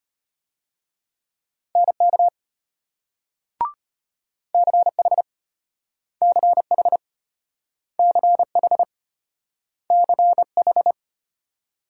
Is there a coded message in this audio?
Yes: Morse code from 1.8 to 10.9 s, heard as a keyed tone, about 34 wpm, 700 Hz. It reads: NK CH CH CH CH